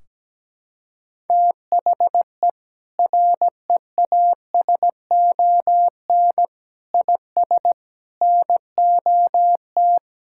Morse code, heard as a keyed tone, about 17 wpm, 700 Hz. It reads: THE REASON IS NOT
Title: Sentences from Top 1600 Words 17wpm